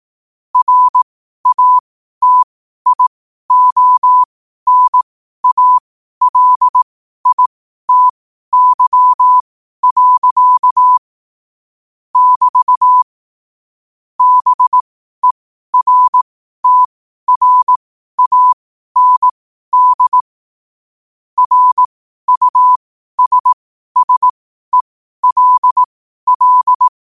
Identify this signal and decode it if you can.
Morse code, heard as a keyed tone, about 18 wpm, spaced as for 12 wpm, 1000 Hz. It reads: RATIONALITY. <BT> BERTRAND RUSSELL
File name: qotdfiles / 12wpm